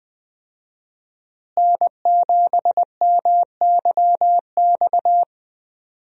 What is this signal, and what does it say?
Morse code, heard as a keyed tone, about 20 wpm, 700 Hz. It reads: N7MYX
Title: USA Call Signs - 5 Characters - Repeated 2x - 20wpm